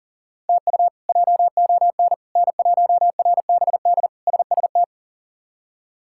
Morse code, heard as a keyed tone, about 40 wpm, 700 Hz. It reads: TU JON N1RBD SST
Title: K1USN SST QSO Practice - 40wpm